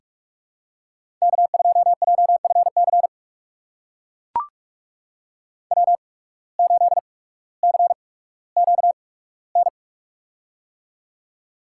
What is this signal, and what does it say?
Morse code, heard as a keyed tone, about 41 wpm, spaced as for 19 wpm, 700 Hz. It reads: K2JFC W8CQN
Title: USA Call Signs - 5 Characters - 8x Character Spacing - 45wpm